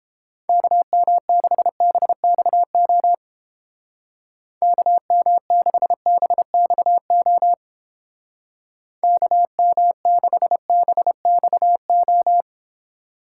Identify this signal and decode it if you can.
Morse code, heard as a keyed tone, about 31 wpm, 700 Hz. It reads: KM6BXO KM6BXO KM6BXO